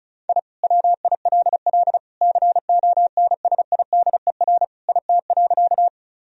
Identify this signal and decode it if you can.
Morse code, heard as a keyed tone, about 34 wpm, 700 Hz. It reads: I WILL CONSIDER IT.
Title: Sentences from Top 1000 Words - Review Part 4 - 35wpm